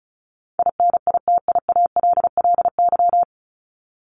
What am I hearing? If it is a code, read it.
Morse code, heard as a keyed tone, about 34 wpm, 700 Hz. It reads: INITIALLY
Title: Top 1251-1500 Words 35wpm